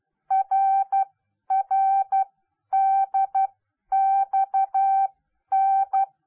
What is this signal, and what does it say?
Morse code, heard as a keyed tone, about 12 wpm, spaced as for 10 wpm, 780 Hz. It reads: RRDXN